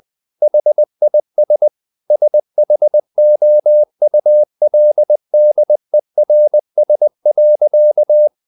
Morse code, heard as a keyed tone, about 20 wpm, 600 Hz. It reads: HIS SHOULDERS.
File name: the old man and the sea / the old man and the sea part 1 20wpm